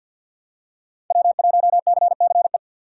Morse code, heard as a keyed tone, about 49 wpm, 700 Hz. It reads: W1CCE
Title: K1USN SST QSO Practice - 50wpm